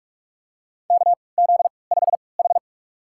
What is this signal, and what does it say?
Morse code, heard as a keyed tone, about 44 wpm, spaced as for 25 wpm, 690 Hz. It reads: KZ5H
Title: USA Call Signs - 4 Characters - 3x Character Spacing - 45wpm